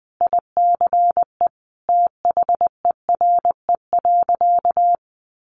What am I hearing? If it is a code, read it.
Morse code, heard as a keyed tone, about 20 wpm, 700 Hz. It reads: ICE THERE.
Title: Sentences from Top 1000 Words 20wpm